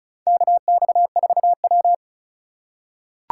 Morse code, heard as a keyed tone, about 36 wpm, 700 Hz. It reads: KX4W
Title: USA Call Signs - 3x Character Spacing - 35wpm